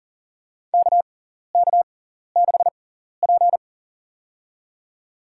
Morse code, heard as a keyed tone, about 40 wpm, spaced as for 14 wpm, 700 Hz. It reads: KK6P
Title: USA Call Signs - 4 Characters - 6x Character Spacing - 40wpm